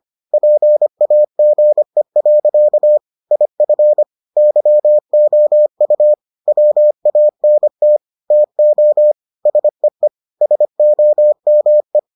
Morse code, heard as a keyed tone, about 25 wpm, 600 Hz. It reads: PAGE. IF YOU WANT TO SEE SOME